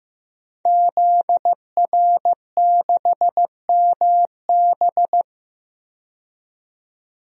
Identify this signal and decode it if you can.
Morse code, heard as a keyed tone, about 15 wpm, 700 Hz. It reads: ZR6MB
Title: DXCC-100 Call Signs - 5 Characters - 15wpm